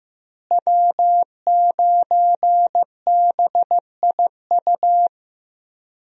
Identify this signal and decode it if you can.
Morse code, heard as a keyed tone, about 15 wpm, 700 Hz. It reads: W9BIU
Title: USA Call Signs - 5 Characters - Repeated 3x - 15wpm